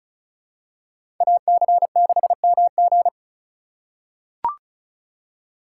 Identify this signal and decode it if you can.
Morse code, heard as a keyed tone, about 34 wpm, 700 Hz. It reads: AC6MG
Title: USA Call Signs - 5 Characters - 3x Character Spacing - 35wpm